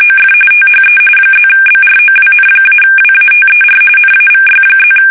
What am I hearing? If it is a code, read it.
RTTY, about 45.45 baud. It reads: OF THE +BROWN-WATER NAVY+
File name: fsk850 / usnavy5